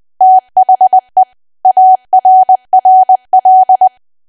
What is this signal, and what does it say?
Morse code, heard as a keyed tone, about 20 wpm, 750 Hz. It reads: THE ARRL
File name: cwa / W1AW-Level3 / 20WPM